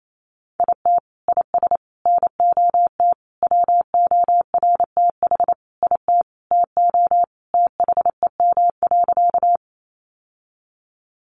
Morse code, heard as a keyed tone, about 28 wpm, 700 Hz. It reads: IT IS NOT WORTH IT TO THEM.
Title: Sentences from Top 800 Words 28wpm